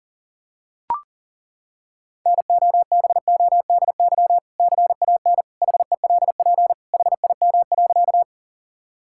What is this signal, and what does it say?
Morse code, heard as a keyed tone, about 37 wpm, 700 Hz. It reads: NOBODY CAN HELP HIM.
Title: Sentences from Top 1600 Words 40wpm